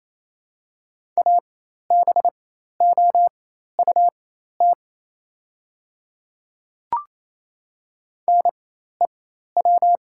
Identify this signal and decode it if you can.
Morse code, heard as a keyed tone, about 27 wpm, spaced as for 13 wpm, 700 Hz. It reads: ABOUT NEW
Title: 4x Character Spacing - Top 300 Words - 28wpm